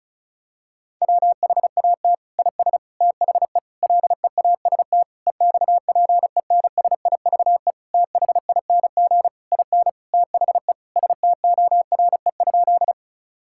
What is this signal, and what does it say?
Morse code, heard as a keyed tone, about 34 wpm, 700 Hz. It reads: WHAT IS THE LEAST EXPENSIVE THING IN THE STORE?
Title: Sentences from Top 1000 Words - Review Part 6 - 35wpm